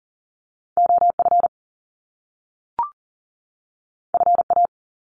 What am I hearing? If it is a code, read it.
Morse code, heard as a keyed tone, about 39 wpm, 700 Hz. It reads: OF FA
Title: Top Two Letter Combinations in Words 40wpm